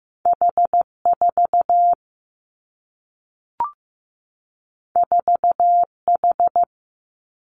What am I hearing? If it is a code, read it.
Morse code, heard as a keyed tone, about 15 wpm, 700 Hz. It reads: H4 4H
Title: Two Challenging Letter-Number Combinations 15wpm